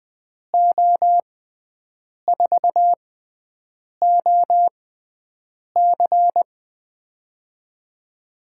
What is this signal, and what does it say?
Morse code, heard as a keyed tone, about 21 wpm, spaced as for 7 wpm, 700 Hz. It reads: O4OC
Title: USA Call Signs - 5 Characters - 6x Character Spacing - 20wpm